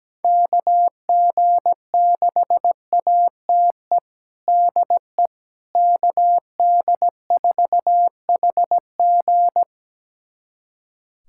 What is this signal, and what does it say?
Morse code, heard as a keyed tone, about 17 wpm, 700 Hz. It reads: KG6ATE DE KD4HG
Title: USA Callsign DE Callsign 17wpm